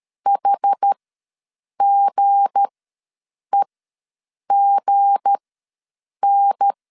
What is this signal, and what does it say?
Morse code, heard as a keyed tone, about 13 wpm, spaced as for 7 wpm, 800 Hz. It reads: HGEGN
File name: LEZIONI / 7 GF9Z / 7 G ws 12wpm 3140